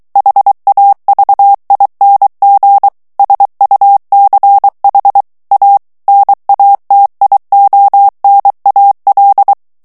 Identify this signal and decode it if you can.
Morse code, heard as a keyed tone, about 23 wpm, 800 Hz. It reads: HAVING SUCH A NATIONAL
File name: morse mp3 / ANGLAIS texte clair 2 vitesse 22 mots minute